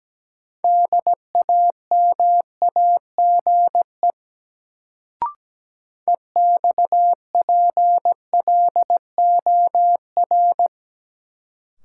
Morse code, heard as a keyed tone, about 17 wpm, 700 Hz. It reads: DAMAGE EXPLOR
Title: Sentences from Top 1100 Words - New Word Drill Alt - 17wpm